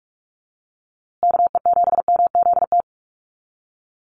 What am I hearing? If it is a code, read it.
Morse code, heard as a keyed tone, about 44 wpm, 700 Hz. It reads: KE7MZT
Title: USA Call Signs Repeated 2x 45wpm